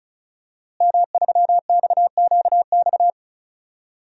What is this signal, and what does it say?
Morse code, heard as a keyed tone, about 36 wpm, 700 Hz. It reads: M3XQX
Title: DXCC-100 Call Signs - 5 Characters - Repeated 3x - 35wpm